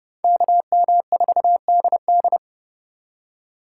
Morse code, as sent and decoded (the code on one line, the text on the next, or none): -.- -- ....- -.. -..
KM4DD